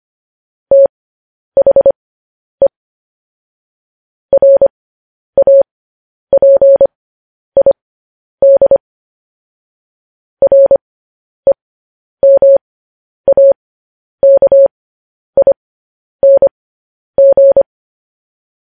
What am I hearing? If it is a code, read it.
Morse code, heard as a keyed tone, about 25 wpm, spaced as for 10 wpm, 550 Hz. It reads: THE RAPID REMAKING
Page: news headlines in morse code at 10 wpm